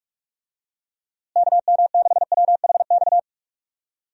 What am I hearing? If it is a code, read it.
Morse code, heard as a keyed tone, about 45 wpm, 700 Hz. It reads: KM6WHX
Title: USA Call Signs - 6 Characters - Repeated 3x - 45wpm